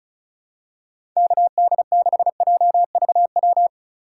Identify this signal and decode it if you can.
Morse code, heard as a keyed tone, about 34 wpm, 700 Hz. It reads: KD6JVW
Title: USA Call Signs - 2x Character Spacing - 35wpm